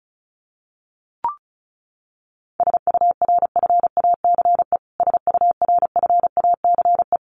Morse code, heard as a keyed tone, about 34 wpm, 700 Hz. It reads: SURFACE SURFACE
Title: Top 500 Words Repeated 35wpm